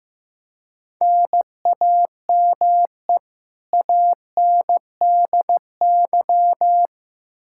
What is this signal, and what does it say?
Morse code, heard as a keyed tone, about 15 wpm, 700 Hz. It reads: NAME ANDY
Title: Practice QSO - Exchange 1 - Beginning - 15wpm